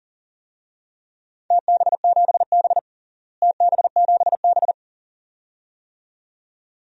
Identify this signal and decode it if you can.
Morse code, heard as a keyed tone, about 40 wpm, 700 Hz. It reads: TB7B TB7B
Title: DXCC-100 Call Signs - 4 Characters - Repeated 2x - 40wpm